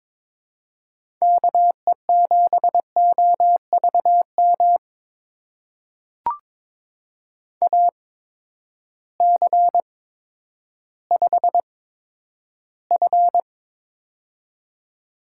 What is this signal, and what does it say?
Morse code, heard as a keyed tone, about 23 wpm, spaced as for 11 wpm, 700 Hz. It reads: KE7OVM AC5F
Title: USA Call Signs - 8x Character Spacing - 22wpm